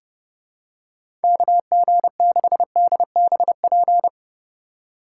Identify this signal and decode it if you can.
Morse code, heard as a keyed tone, about 30 wpm, 700 Hz. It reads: KG6DBP
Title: USA Call Signs - 6 Characters - 3x Character Spacing - 30wpm